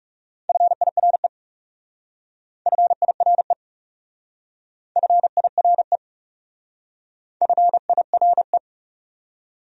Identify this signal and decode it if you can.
Morse code, heard as a keyed tone, about 35 wpm, 700 Hz. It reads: FIRE FIRE FIRE FIRE
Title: Speed-Racing Top 500 Words 30wpm